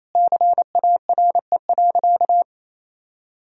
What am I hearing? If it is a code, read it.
Morse code, heard as a keyed tone, about 27 wpm, 700 Hz. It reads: CARE.
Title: Sentences from Top 1000 Words - Review Part 2 - 28wpm